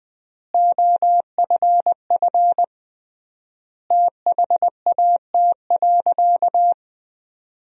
Morse code, heard as a keyed tone, about 21 wpm, 700 Hz. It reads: OFF THAT.
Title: 3x Word-Spacing - Sentences from Top 400 Words - 20wpm